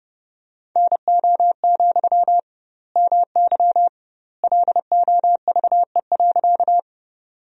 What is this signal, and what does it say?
Morse code, heard as a keyed tone, about 30 wpm, 700 Hz. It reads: NO, MY LOVE.
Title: 2x Word-Spacing - Sentences from Top 400 Words - 30wpm